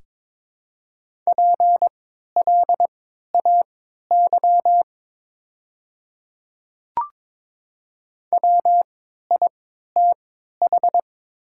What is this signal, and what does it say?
Morse code, heard as a keyed tone, about 22 wpm, spaced as for 13 wpm, 700 Hz. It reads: PLAY WITH
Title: 3x Character Spacing - Top 200 Words - 22wpm